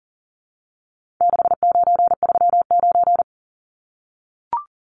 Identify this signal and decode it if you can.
Morse code, heard as a keyed tone, about 40 wpm, 700 Hz. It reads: 6939
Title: Numbers - 4-Digit 40wpm